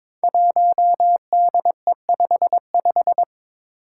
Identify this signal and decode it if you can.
Morse code, heard as a keyed tone, about 22 wpm, 700 Hz. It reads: 1DE55
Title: Five Challenging Letter-Number Combinations 22wpm